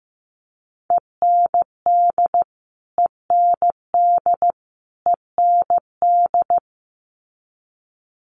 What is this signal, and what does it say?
Morse code, heard as a keyed tone, about 15 wpm, 700 Hz. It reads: END END END
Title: Top 500 Words Repeated 15wpm